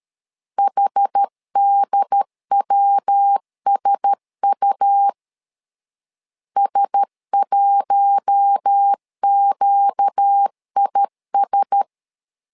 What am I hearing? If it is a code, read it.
Morse code, heard as a keyed tone, about 13 wpm, 800 Hz. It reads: HDWSU S1QIS
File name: LEZIONI / 4 Q1S 12wpm 1802